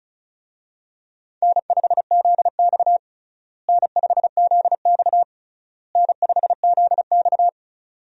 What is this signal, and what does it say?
Morse code, heard as a keyed tone, about 36 wpm, 700 Hz. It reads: N5ZX N5ZX N5ZX
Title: USA Call Signs - 4 Characters - Repeated 3x - 35wpm